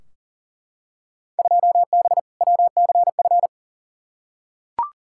Morse code, heard as a keyed tone, about 40 wpm, 700 Hz. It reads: 2B WCF
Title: Field Day Exchanges Encore - 40wpm